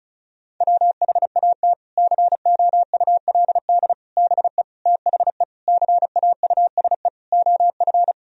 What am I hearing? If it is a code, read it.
Morse code, heard as a keyed tone, about 36 wpm, 700 Hz. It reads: WHAT COULD BE THE CAUSE OF
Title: Sentences from Top 1000 Words - Review Part 1 - 35wpm